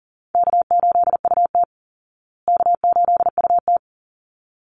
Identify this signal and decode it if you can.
Morse code, heard as a keyed tone, about 40 wpm, 700 Hz. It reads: K8UT K8UT
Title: Easy Call Signs Repeated 2x - 40wpm